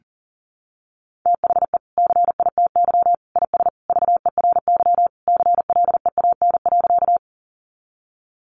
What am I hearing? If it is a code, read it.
Morse code, heard as a keyed tone, about 40 wpm, 700 Hz. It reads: THE CITY IS VERY CLEAN.
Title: Easy Phrases 40wpm